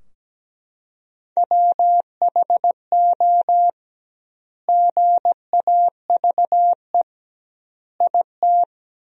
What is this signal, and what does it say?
Morse code, heard as a keyed tone, about 17 wpm, 700 Hz. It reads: WHO GAVE IT
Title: 2x Word-Spacing - Sentences from Top 500 Words - 17wpm